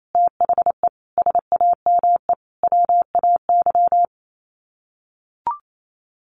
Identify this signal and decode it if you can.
Morse code, heard as a keyed tone, about 28 wpm, 700 Hz. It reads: THE SAME WAY
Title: Sets of 4 Words 28wpm